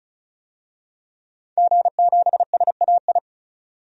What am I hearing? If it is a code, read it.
Morse code, heard as a keyed tone, about 36 wpm, 700 Hz. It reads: G7SAI
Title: DXCC-100 Call Signs - 5 Characters - 35wpm